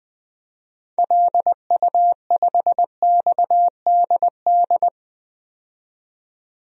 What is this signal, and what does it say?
Morse code, heard as a keyed tone, about 20 wpm, 700 Hz. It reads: LU5XDD
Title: DXCC-100 Call Signs - 6 Characters - 20wpm